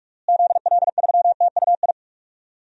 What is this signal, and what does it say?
Morse code, heard as a keyed tone, about 47 wpm, 700 Hz. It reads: ZL3TUI
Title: DXCC-100 Call Signs - 6 Characters - Repeated 2x - 45wpm